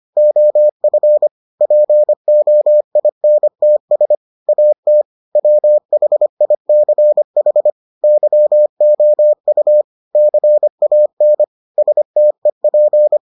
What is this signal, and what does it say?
Morse code, heard as a keyed tone, about 25 wpm, 600 Hz. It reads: OF POINTS AT WHICH YOU CAN STEP